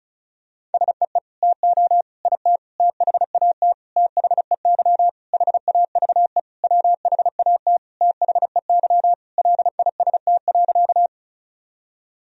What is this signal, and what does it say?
Morse code, heard as a keyed tone, about 34 wpm, 700 Hz. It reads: SEE TO IT THAT THEY HAVE WHAT THEY LIST.
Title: Sentences from Top 1000 Words - Review Part 2 - 35wpm